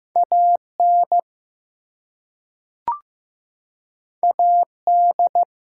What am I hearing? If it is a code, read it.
Morse code, heard as a keyed tone, about 15 wpm, 700 Hz. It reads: AN AD
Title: Two Letter Words - 15wpm